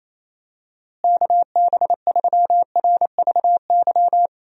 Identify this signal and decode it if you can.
Morse code, heard as a keyed tone, about 28 wpm, 700 Hz. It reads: KB3RVY
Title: USA Call Signs - 6 Characters - Repeated 3x - 28wpm